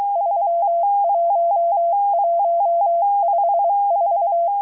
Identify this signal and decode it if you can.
Morse code, heard as a keyed tone, about 23 wpm, 680 Hz. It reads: 31154